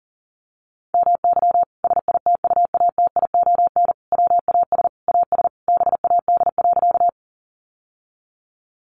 Morse code, heard as a keyed tone, about 40 wpm, 700 Hz. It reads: MY SITUATION WAS AS BAD.
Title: Sentences from Top 1200 Words 40wpm